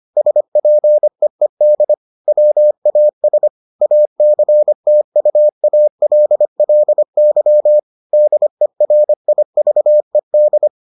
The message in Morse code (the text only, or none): SPEED WAS ACTUALLY DERIVED